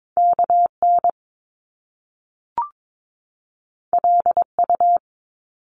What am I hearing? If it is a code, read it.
Morse code, heard as a keyed tone, about 22 wpm, 700 Hz. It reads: KN LU
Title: Top Two Letter Combinations in Words 22wpm